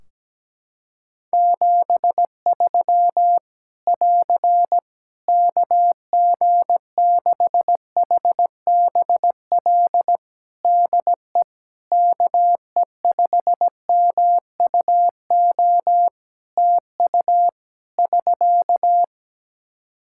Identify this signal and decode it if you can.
Morse code, heard as a keyed tone, about 17 wpm, 700 Hz. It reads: 73 <AR> KG6HBL DE KE5MUO TU <SK>
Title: Practice QSO - Exchange 4 - Ending - 17wpm